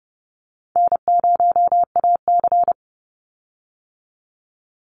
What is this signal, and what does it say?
Morse code, heard as a keyed tone, about 30 wpm, 700 Hz. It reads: N0AC
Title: Easy Call Signs - 30wpm